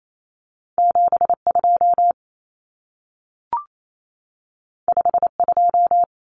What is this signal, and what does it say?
Morse code, heard as a keyed tone, about 28 wpm, 700 Hz. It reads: 72 52
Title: Common Cut Numbers - 2-Digit 28wpm